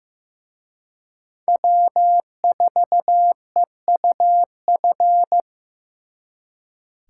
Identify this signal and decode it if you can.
Morse code, heard as a keyed tone, about 18 wpm, 700 Hz. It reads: W4EUF
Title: Speed Racing - USA 1x3 Call Signs 15wpm